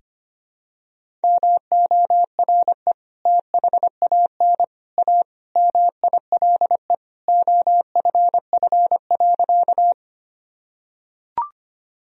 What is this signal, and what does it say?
Morse code, heard as a keyed tone, about 25 wpm, 700 Hz. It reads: MORE THAN A MILE OFF.
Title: Sentences from Top 1000 Words - Review Part 1 - 25wpm